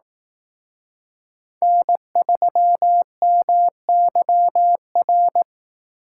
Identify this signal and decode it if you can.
Morse code, heard as a keyed tone, about 18 wpm, 700 Hz. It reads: N3MYR
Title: Speed Racing - USA 1x3 Call Signs 15wpm